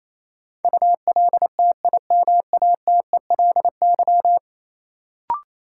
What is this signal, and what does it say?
Morse code, heard as a keyed tone, about 28 wpm, 700 Hz. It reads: ULTIMATELY
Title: Sentences from Top 1200 Words - New Word Drill Alt - 28wpm